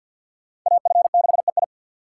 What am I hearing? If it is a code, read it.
Morse code, heard as a keyed tone, about 60 wpm, 700 Hz. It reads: AF6EI
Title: Speed Racing - USA 2x2 Call Signs 50wpm